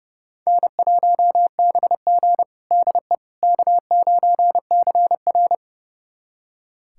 Morse code, heard as a keyed tone, about 30 wpm, 700 Hz. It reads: N1BG DE K9CR
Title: USA Callsign DE Callsign - 4 Characters - 30wpm